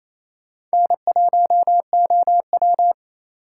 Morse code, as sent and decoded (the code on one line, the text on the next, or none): -. .---- --- .--
N1OW